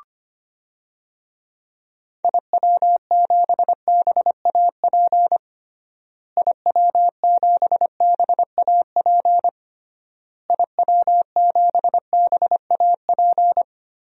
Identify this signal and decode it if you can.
Morse code, heard as a keyed tone, about 25 wpm, 700 Hz. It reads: IW7BAP IW7BAP IW7BAP
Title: DXCC-100 Call Signs - 6 Characters - Repeated 3x - 25wpm